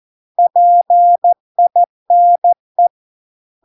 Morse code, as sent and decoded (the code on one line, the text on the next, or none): .--. .. -. .
PINE